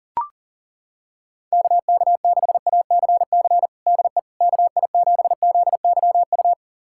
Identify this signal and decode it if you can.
Morse code, heard as a keyed tone, about 40 wpm, 700 Hz. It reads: KK6ACC DE KI7ZYU
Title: USA Callsign DE Callsign - 6 Characters - 40wpm